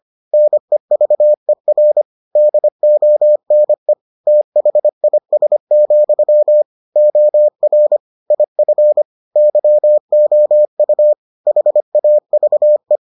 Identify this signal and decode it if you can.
Morse code, heard as a keyed tone, about 25 wpm, 600 Hz. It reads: NEVER DONE THIS, OR IF YOU HAVE